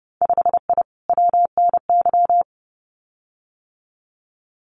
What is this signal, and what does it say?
Morse code, heard as a keyed tone, about 30 wpm, 700 Hz. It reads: 5I WNY
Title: Winter Field Day Exchanges Encore 30wpm